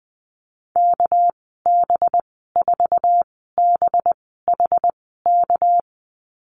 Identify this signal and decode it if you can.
Morse code, heard as a keyed tone, about 20 wpm, spaced as for 15 wpm, 700 Hz. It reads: KB4BHK
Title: 245 Lesson 39g - Letter z - Call Signs 20wpm